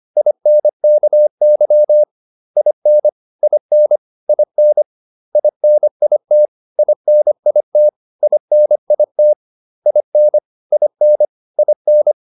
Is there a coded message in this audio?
Yes: Morse code from 0.2 to 12.1 s, heard as a keyed tone, about 25 wpm, 600 Hz. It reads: INKY IN IN IN INIT INIT INIT IN IN IN